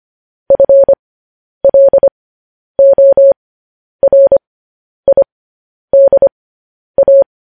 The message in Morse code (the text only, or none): FLORIDA